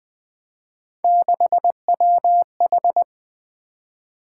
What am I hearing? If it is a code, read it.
Morse code, heard as a keyed tone, about 20 wpm, 700 Hz. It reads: 6WH
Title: Three Letter-Number Combinations - 20wpm